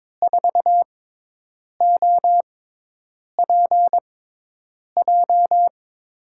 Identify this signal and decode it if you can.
Morse code, heard as a keyed tone, about 22 wpm, spaced as for 8 wpm, 700 Hz. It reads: 4OPJ
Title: USA Call Signs - 6 Characters - 6x Character Spacing - 22wpm